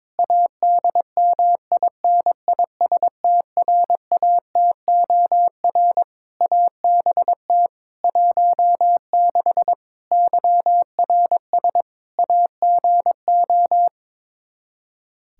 Morse code, heard as a keyed tone, about 22 wpm, 700 Hz. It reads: ADMINISTRATOR ABT 16 YRS AGO
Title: Practice QSO - Exchange 3 - More Info - 22wpm